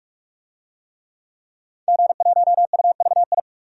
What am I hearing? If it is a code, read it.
Morse code, heard as a keyed tone, about 44 wpm, 700 Hz. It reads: G1UVI
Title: DXCC-100 Call Signs - 5 Characters - 45wpm